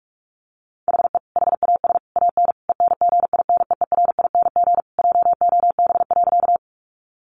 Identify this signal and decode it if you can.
Morse code, heard as a keyed tone, about 44 wpm, 700 Hz. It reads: HE HAS AN ENGINEERING JOB.
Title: Sentences from Top 1500 Words 45wpm